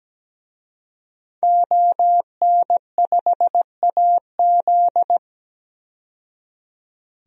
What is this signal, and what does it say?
Morse code, heard as a keyed tone, about 17 wpm, 700 Hz. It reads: ON5AZ
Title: DXCC-100 Call Signs - 5 Characters - 17wpm